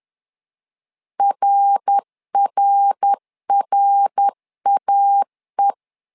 Morse code, heard as a keyed tone, about 10 wpm, 800 Hz. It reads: RRRAE
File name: LEZIONI / 3 MWRO / 3 MWR 10wpm 1952